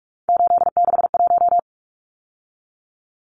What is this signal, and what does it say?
Morse code, heard as a keyed tone, about 44 wpm, 700 Hz. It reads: 861
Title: Numbers - 3-Digit 45wpm